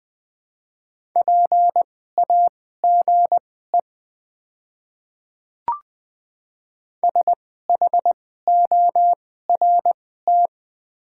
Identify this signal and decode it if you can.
Morse code, heard as a keyed tone, about 21 wpm, spaced as for 15 wpm, 690 Hz. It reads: PAGE SHORT
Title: 2x Character Spacing - Top 500 Words - 20wpm